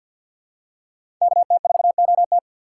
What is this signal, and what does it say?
Morse code, heard as a keyed tone, about 53 wpm, 690 Hz. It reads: KT4OT
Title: USA Call Signs - 5 Characters - 4x Character Spacing - 50wpm